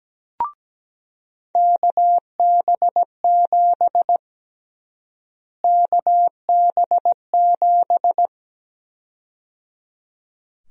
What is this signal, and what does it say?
Morse code, heard as a keyed tone, about 17 wpm, 700 Hz. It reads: KB7 KB7
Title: USA Call Sign Prefixes - Repeated 2x - 17wpm